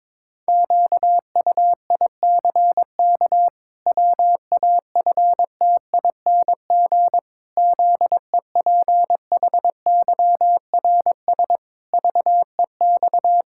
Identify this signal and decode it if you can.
Morse code, heard as a keyed tone, about 22 wpm, 700 Hz. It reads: QUICK WAFTING ZEPHYRS VEX
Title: Pangram Challenge v2 - 22wpm